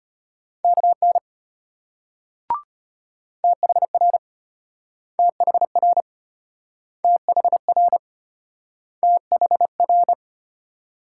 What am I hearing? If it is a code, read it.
Morse code, heard as a keyed tone, about 34 wpm, 700 Hz. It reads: KN THR THR THR THR